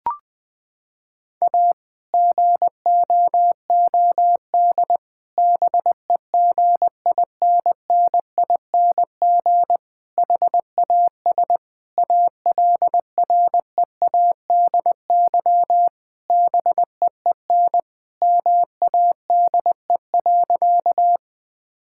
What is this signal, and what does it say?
Morse code, heard as a keyed tone, about 20 wpm, 700 Hz. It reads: A GOOD BEGINNING HAS ALREADY BEEN MADE.
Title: Sentences from Top 1000 Words - Review Part 5 - 20wpm